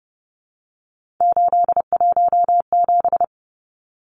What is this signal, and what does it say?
Morse code, heard as a keyed tone, about 30 wpm, 700 Hz. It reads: 817
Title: Numbers - 3-Digit 30wpm